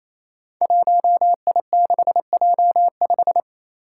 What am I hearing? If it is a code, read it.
Morse code, heard as a keyed tone, about 28 wpm, 700 Hz. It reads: 1I6J5
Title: Five Challenging Letter-Number Combinations 28wpm